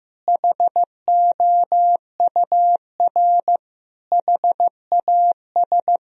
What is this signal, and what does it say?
Morse code, heard as a keyed tone, about 16 wpm, 700 Hz. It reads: HOUR HAS
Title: Sentences from Top 1000 Words - Review Part 1 - 15wpm